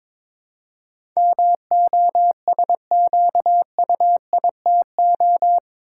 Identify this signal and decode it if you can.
Morse code, heard as a keyed tone, about 22 wpm, 700 Hz. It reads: MOSQUITO
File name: practice / 22wpm / Eight Letter Words - 22wpm